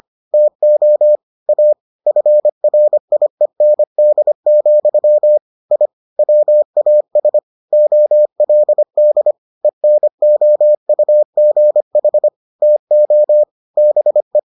Morse code, heard as a keyed tone, about 25 wpm, 600 Hz. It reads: TO A FRIEND, I WAS OLD ENOUGH TO BE